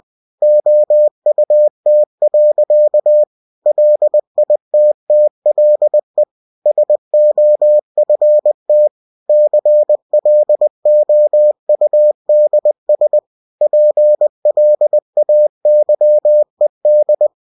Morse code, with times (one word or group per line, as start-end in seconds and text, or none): 0.4-3.2 OUT.
3.7-6.2 LITTLE
6.7-8.9 SOFT
9.3-13.2 CLOUDS
13.6-17.3 PLAYED